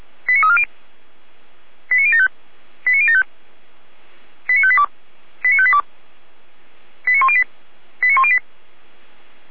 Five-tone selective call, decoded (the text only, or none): 89340 89075 89075 89582 89582 89108 89108